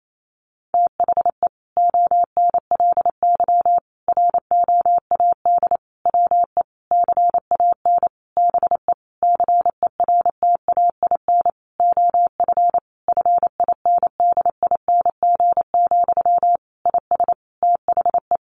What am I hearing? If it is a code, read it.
Morse code, heard as a keyed tone, about 28 wpm, 700 Hz. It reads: THE ONLY ROAD WE CAN BE CERTAIN OF FINDING, IS THE
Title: Sentences from Top 1100 Words 28wpm